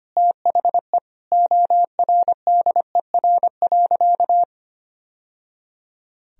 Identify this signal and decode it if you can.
Morse code, heard as a keyed tone, about 25 wpm, 700 Hz. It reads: THE ORDER.
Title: Sentences from Top 1000 Words - Review Part 2 - 25wpm